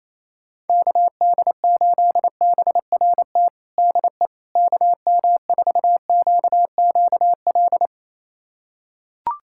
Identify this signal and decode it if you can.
Morse code, heard as a keyed tone, about 28 wpm, 700 Hz. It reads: KD8BRT DE KM4QQL
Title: USA Callsign DE Callsign 28wpm